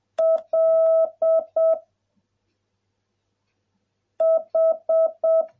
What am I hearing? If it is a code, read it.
Morse code, heard as a keyed tone, about 7 wpm, 660 Hz. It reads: L H